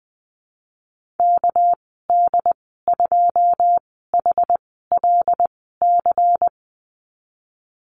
Morse code, heard as a keyed tone, about 20 wpm, spaced as for 15 wpm, 700 Hz. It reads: KD2HLC
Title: 207 Lesson 33g - Letter j - Call Signs 20wpm